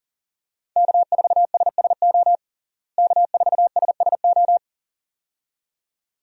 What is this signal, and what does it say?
Morse code, heard as a keyed tone, about 37 wpm, 700 Hz. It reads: K4SSO K4SSO
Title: USA Call Signs - 5 Characters - Repeated 2x - 40wpm